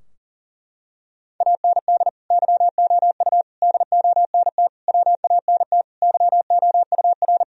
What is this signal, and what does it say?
Morse code, heard as a keyed tone, about 37 wpm, 700 Hz. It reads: AND YOU DONT WANT YOUR
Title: Sentences from Top 1000 Words - Review Part 2 - 40wpm